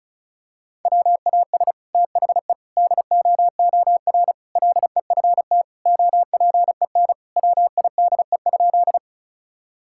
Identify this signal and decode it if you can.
Morse code, heard as a keyed tone, about 34 wpm, 700 Hz. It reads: WAS THE DOOR LEFT OPEN WIDE?
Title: Sentences from Top 1000 Words - Review Part 6 - 35wpm